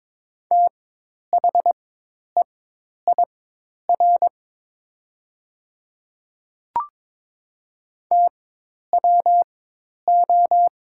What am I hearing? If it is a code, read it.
Morse code, heard as a keyed tone, about 22 wpm, spaced as for 10 wpm, 700 Hz. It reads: THEIR TWO
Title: 4x Character Spacing - Top 100 Words - 22wpm